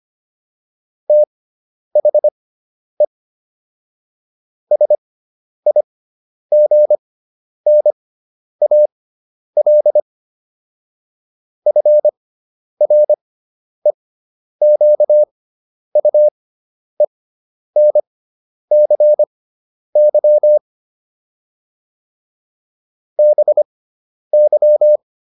Morse code, heard as a keyed tone, about 25 wpm, spaced as for 10 wpm, 600 Hz. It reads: THE SIGNAL FREQUENCY BY